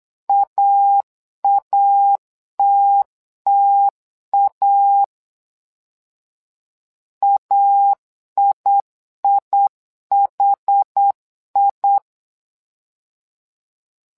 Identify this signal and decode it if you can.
Morse code, heard as a keyed tone, about 8 wpm, 800 Hz. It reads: AATTA AIIHI
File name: LEZIONI / 2 UHDN / 2 UH 8wpm 2129